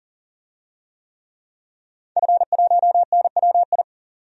Morse code, heard as a keyed tone, about 40 wpm, 700 Hz. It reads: F1NWI
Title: DXCC-100 Call Signs - 5 Characters - 40wpm